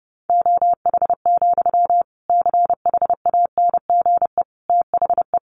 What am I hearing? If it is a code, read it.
Morse code, heard as a keyed tone, about 30 wpm, 700 Hz. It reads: OH, CHANGE THE
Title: Sentences from Top 800 Words 30wpm